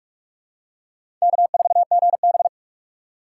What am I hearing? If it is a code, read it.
Morse code, heard as a keyed tone, about 46 wpm, 700 Hz. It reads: K4GB
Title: USA Call Signs - 4 Characters - 2x Character Spacing - 45wpm